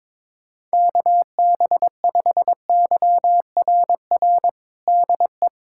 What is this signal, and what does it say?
Morse code, heard as a keyed tone, about 22 wpm, 700 Hz. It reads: KB5YRR DE
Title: USA Callsign DE Callsign 22wpm